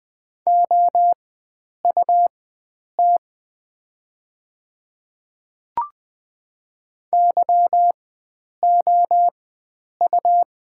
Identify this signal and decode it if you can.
Morse code, heard as a keyed tone, about 20 wpm, spaced as for 9 wpm, 690 Hz. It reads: OUT YOU
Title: 4x Character Spacing - Top 100 Words - 20wpm